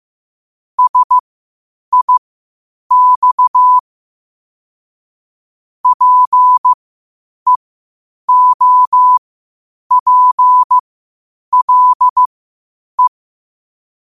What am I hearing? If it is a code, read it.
Morse code, heard as a keyed tone, about 15 wpm, spaced as for 8 wpm, 1000 Hz. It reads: SIX PEOPLE
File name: qotdfiles / 8wpm